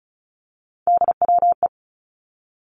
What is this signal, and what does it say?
Morse code, heard as a keyed tone, about 36 wpm, 700 Hz. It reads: DWE
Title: Three Challenging Letter-Number Combinations 35wpm